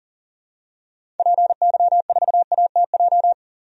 Morse code, heard as a keyed tone, about 40 wpm, 700 Hz. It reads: PY4ATJ
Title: DXCC-100 Call Signs - 6 Characters - 40wpm